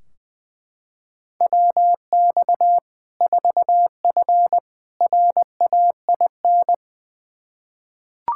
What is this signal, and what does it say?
Morse code, heard as a keyed tone, about 20 wpm, 700 Hz. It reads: WX 4F RAIN
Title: Practice QSO - Exchange 2 - Info - 20wpm